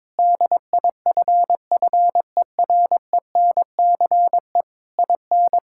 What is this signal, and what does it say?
Morse code, heard as a keyed tone, about 21 wpm, 700 Hz. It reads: DIFFERENCE IN
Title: Sentences from Top 1000 Words - Review Part 5 - 22wpm